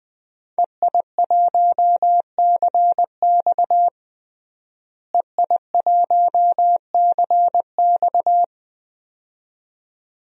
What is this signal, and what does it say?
Morse code, heard as a keyed tone, about 20 wpm, 700 Hz. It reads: EI1CX EI1CX
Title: DXCC-100 Call Signs - 5 Characters - Repeated 2x - 20wpm